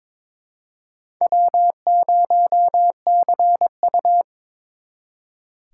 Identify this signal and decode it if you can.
Morse code, heard as a keyed tone, about 28 wpm, 700 Hz. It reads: W0CU
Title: Speed Racing - USA Call Signs 22wpm